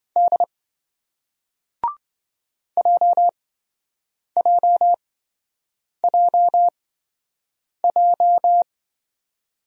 Morse code, heard as a keyed tone, about 26 wpm, spaced as for 7 wpm, 700 Hz. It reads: D JJJJ